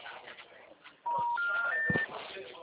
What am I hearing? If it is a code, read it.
Special information tone, with noise.
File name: out- -2002- - - .3227